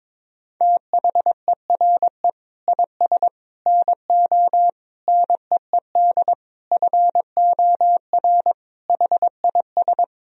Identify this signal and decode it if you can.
Morse code, heard as a keyed tone, about 22 wpm, 700 Hz. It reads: THERE IS NO NEED FOR HIS
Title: Sentences from Top 1000 Words - Review Part 3 - 22wpm